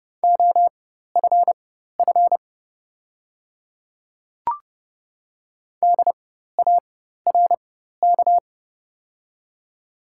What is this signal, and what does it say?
Morse code, heard as a keyed tone, about 30 wpm, spaced as for 14 wpm, 700 Hz. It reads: OFF DARK
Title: 4x Character Spacing - Top 500 Words - 30wpm